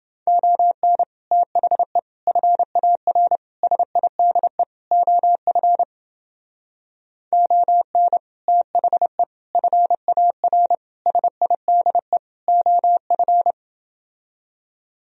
Morse code, heard as a keyed tone, about 28 wpm, 700 Hz. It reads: ON THE FAR SIDE OF ON THE FAR SIDE OF